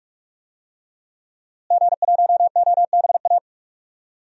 Morse code, heard as a keyed tone, about 42 wpm, 700 Hz. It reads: G1OBA
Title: DXCC-100 Call Signs - 5 Characters - Repeated 3x - 45wpm